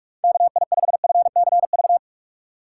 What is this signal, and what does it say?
Morse code, heard as a keyed tone, about 56 wpm, 700 Hz. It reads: KI5FCV